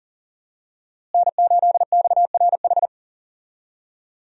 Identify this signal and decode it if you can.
Morse code, heard as a keyed tone, about 40 wpm, 700 Hz. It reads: N8XRH
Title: USA Call Signs - 5 Characters - Repeated 3x - 40wpm